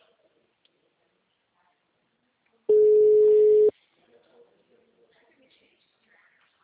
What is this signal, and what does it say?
Ringback tone (Europe, 425 Hz), with speech.
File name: out- -2011- - - .5542